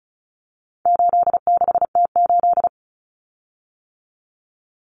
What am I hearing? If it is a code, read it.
Morse code, heard as a keyed tone, about 34 wpm, 700 Hz. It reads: 86T8
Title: Common Cut Numbers - 4-Digit 35wpm